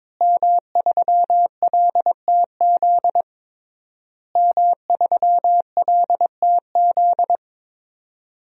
Morse code, heard as a keyed tone, about 22 wpm, 700 Hz. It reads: M3LTZ M3LTZ
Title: DXCC-100 Call Signs - 5 Characters - Repeated 3x - 22wpm